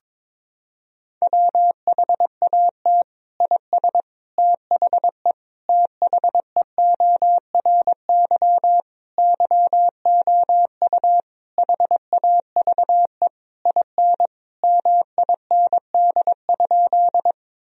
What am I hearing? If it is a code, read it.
Morse code, heard as a keyed tone, about 22 wpm, 700 Hz. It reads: WHAT IS THE THEORY YOU HAVE IN MIND?
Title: Sentences from Top 1000 Words - Review Part 3 - 22wpm